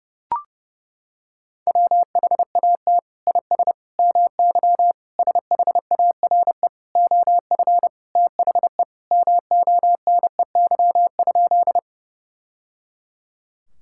Morse code, heard as a keyed tone, about 30 wpm, 700 Hz. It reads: WHAT IS MY SHARE OF THE MONEY?
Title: Sentences from Top 1000 Words - Review Part 6 - 30wpm